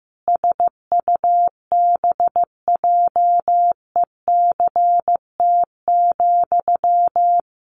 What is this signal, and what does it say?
Morse code, heard as a keyed tone, about 15 wpm, 700 Hz. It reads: SUBJECT,
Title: Sentences from Top 800 Words 15wpm